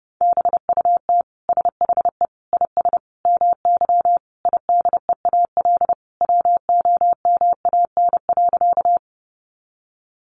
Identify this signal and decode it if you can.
Morse code, heard as a keyed tone, about 30 wpm, 700 Hz. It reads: BUT SHE IS MY IDEAL WOMAN.
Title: Sentences from Top 1400 Words 30wpm